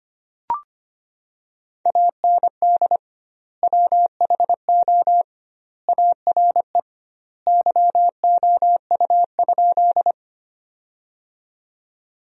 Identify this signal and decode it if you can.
Morse code, heard as a keyed tone, about 25 wpm, 700 Hz. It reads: AND WHO ARE YOU?
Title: 2x Word-Spacing - Sentences from Top 100 Words - 25wpm